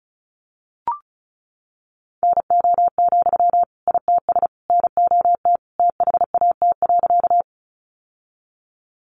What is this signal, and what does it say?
Morse code, heard as a keyed tone, about 34 wpm, 700 Hz. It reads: NO, ITS NOT THAT.
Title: Sentences from Top 600 Words 35wpm